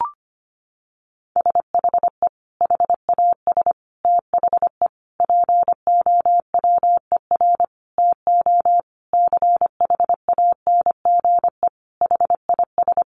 Morse code, heard as a keyed tone, about 25 wpm, 700 Hz. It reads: SHE HAS THE POWER TO CHANGE HIS
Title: Sentences from Top 1200 Words 25wpm